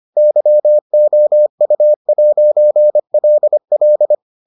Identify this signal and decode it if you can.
Morse code, heard as a keyed tone, about 25 wpm, 600 Hz. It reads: YOU'LL